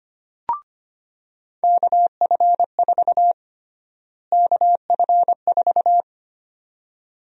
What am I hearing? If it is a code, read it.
Morse code, heard as a keyed tone, about 25 wpm, 700 Hz. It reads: KF4 KF4
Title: USA Call Sign Prefixes - Repeated 2x - 25wpm